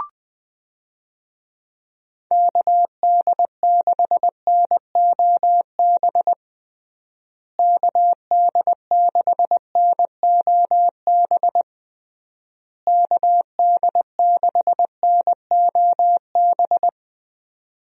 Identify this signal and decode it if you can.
Morse code, heard as a keyed tone, about 20 wpm, 700 Hz. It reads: KD6NOB KD6NOB KD6NOB
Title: USA Call Signs - 6 Characters - Repeated 3x - 20wpm